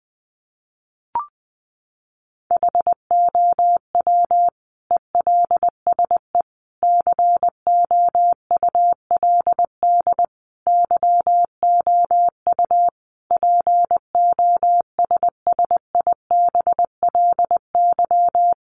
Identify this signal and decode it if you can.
Morse code, heard as a keyed tone, about 20 wpm, 700 Hz. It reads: HOW ELSE COULD YOU POSSIBLY
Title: Sentences from Top 1300 Words 20wpm